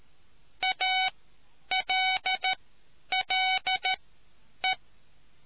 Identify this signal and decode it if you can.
Morse code, heard as a keyed tone, about 13 wpm, spaced as for 9 wpm, 750 Hz. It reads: ALLE